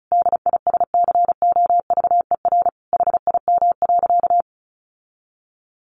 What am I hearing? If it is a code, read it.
Morse code, heard as a keyed tone, about 34 wpm, 700 Hz. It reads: DISCOVER HIM.
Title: Sentences from Top 1100 Words 35wpm